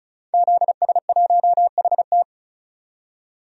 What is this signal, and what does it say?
Morse code, heard as a keyed tone, about 36 wpm, 700 Hz. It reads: ZS1HT
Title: DXCC-100 Call Signs - 5 Characters - 35wpm